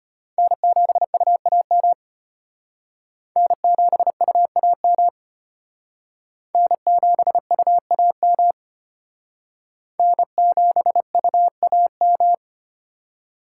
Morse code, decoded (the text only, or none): N7UAM N7UAM N7UAM N7UAM